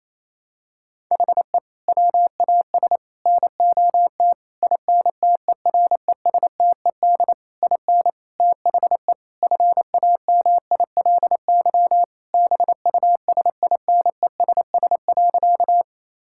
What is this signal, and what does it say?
Morse code, heard as a keyed tone, about 27 wpm, 700 Hz. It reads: HE WAS NOT INTERESTED IN THE FAMILY BUSINESS.
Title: Sentences from Top 1000 Words - Review Part 6 - 28wpm